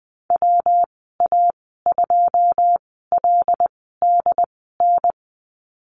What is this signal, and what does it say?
Morse code, heard as a keyed tone, about 20 wpm, spaced as for 14 wpm, 700 Hz. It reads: WA2LDN
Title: 140 Lesson 23g - Letter f - Call Signs 20wpm